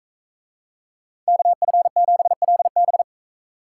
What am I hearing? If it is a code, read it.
Morse code, heard as a keyed tone, about 39 wpm, 700 Hz. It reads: KF7LB